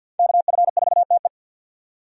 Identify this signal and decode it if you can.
Morse code, heard as a keyed tone, about 52 wpm, 700 Hz. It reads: KF4TE
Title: USA Call Signs - 5 Characters - 50wpm